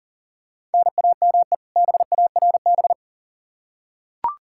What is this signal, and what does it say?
Morse code, heard as a keyed tone, about 40 wpm, 700 Hz. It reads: NAME BARB
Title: K1USN SST QSO Practice 40wpm